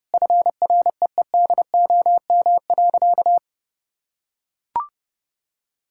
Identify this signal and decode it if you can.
Morse code, heard as a keyed tone, about 30 wpm, 700 Hz. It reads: FREEDOM.
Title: Sentences from Top 1000 Words - Review Part 5 - 30wpm